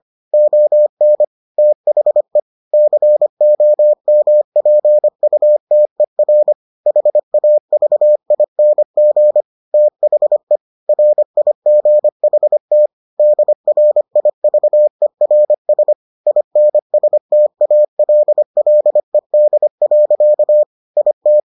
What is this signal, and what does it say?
Morse code, heard as a keyed tone, about 25 wpm, 600 Hz. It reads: ON THE COMPUTER HAVING THE RIGHT DRIVERS INSTALLED. IT